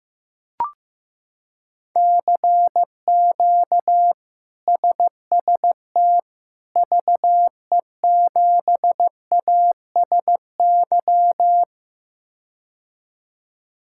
Morse code, heard as a keyed tone, about 15 wpm, 700 Hz. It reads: CQ SST VE7ASY
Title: K1USN SST QSO Practice - 15wpm